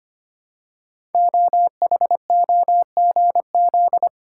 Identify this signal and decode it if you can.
Morse code, heard as a keyed tone, about 25 wpm, 700 Hz. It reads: OHOGZ
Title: Code Groups - 5 Letters - 25wpm